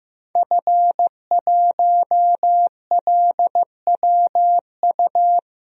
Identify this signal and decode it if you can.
Morse code, heard as a keyed tone, about 15 wpm, 700 Hz. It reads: F1LWU
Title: DXCC-100 Call Signs - 5 Characters - 15wpm